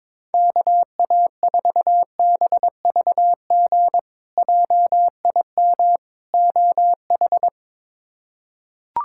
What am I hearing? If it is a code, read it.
Morse code, heard as a keyed tone, about 21 wpm, 700 Hz. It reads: KA4BVG JIM OH
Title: K1USN SST QSO Practice - 22wpm